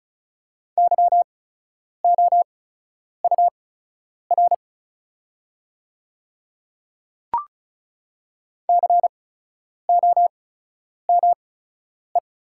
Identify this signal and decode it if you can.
Morse code, heard as a keyed tone, about 35 wpm, spaced as for 10 wpm, 690 Hz. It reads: YOUR COME
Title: 8x Character Spacing - Top 100 Words - 35wpm